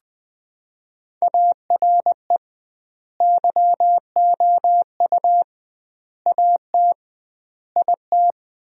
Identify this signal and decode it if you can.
Morse code, heard as a keyed tone, about 20 wpm, 700 Hz. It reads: ARE YOU AT IT
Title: 2x Word-Spacing - Sentences from Top 200 Words - 20wpm